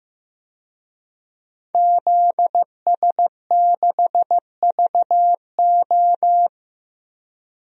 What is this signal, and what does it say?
Morse code, heard as a keyed tone, about 15 wpm, 700 Hz. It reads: ZS6VO
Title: DXCC-100 Call Signs - 5 Characters - Repeated 2x - 15wpm